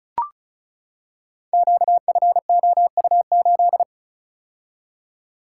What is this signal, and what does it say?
Morse code, heard as a keyed tone, about 36 wpm, 700 Hz. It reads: QFOU8
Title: Code Groups - 5 Letter:Numbers - 35wpm